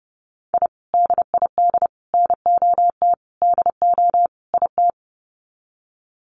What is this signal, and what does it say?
Morse code, heard as a keyed tone, about 30 wpm, 700 Hz. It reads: I DID NOT DO IT
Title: Sets of 5 Words Encore2 30wpm